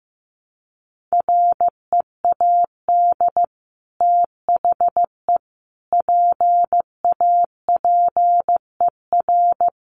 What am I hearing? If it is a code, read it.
Morse code, heard as a keyed tone, about 15 wpm, 700 Hz. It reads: READ THE PAPER
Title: Sentences from Top 600 Words 15wpm